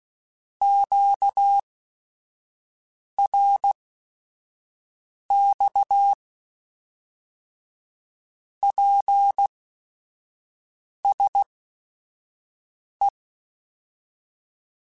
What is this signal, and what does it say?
Morse code, heard as a keyed tone, about 16 wpm, spaced as for 5 wpm, 800 Hz. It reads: QRX PSE